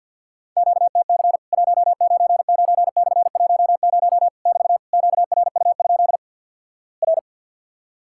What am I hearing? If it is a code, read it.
Morse code, heard as a keyed tone, about 50 wpm, 700 Hz. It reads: QTC 199/10 <BT> QRU? R
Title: Worked All Europe DX Contest - QTC Exchange with Repeats - 50wpm